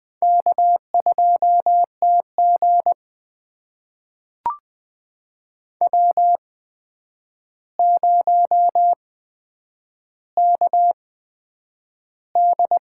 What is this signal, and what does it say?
Morse code, heard as a keyed tone, about 20 wpm, spaced as for 9 wpm, 700 Hz. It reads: K2TG W0KD
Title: USA Call Signs - 8x Character Spacing - 20wpm